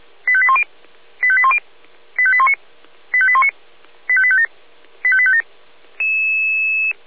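Five-tone selective call (ZVEI), heard as quick runs of five tones, tones of about 70 ms. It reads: 85810 85810 85819 85819 85857 85857
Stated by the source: ZVEI five-tone selective call received on SDR.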